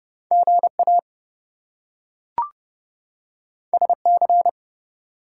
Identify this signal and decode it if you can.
Morse code, heard as a keyed tone, about 30 wpm, 700 Hz. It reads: GA SC
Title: US States and Canadian Provinces 30wpm